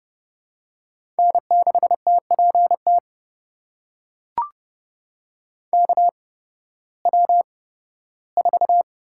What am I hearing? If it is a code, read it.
Morse code, heard as a keyed tone, about 29 wpm, spaced as for 16 wpm, 700 Hz. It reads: N6TPT KW4
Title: USA Call Signs - 5 Characters - 8x Character Spacing - 30wpm